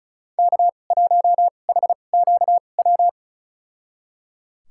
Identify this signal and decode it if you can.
Morse code, heard as a keyed tone, about 34 wpm, spaced as for 25 wpm, 700 Hz. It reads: K1HQW
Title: USA Call Signs - 5 Characters - 2x Character Spacing - 35wpm